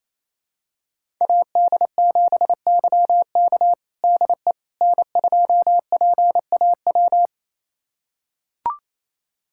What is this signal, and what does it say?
Morse code, heard as a keyed tone, about 28 wpm, 700 Hz. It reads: AD7YK DE N2PAW
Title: USA Callsign DE Callsign - 5 Characters - 28wpm